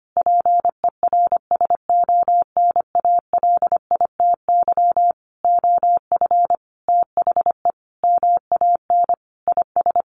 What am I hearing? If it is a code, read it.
Morse code, heard as a keyed tone, about 25 wpm, 700 Hz. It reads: PERSONALITY OF THE MAN IS
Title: Sentences from Top 1400 Words 25wpm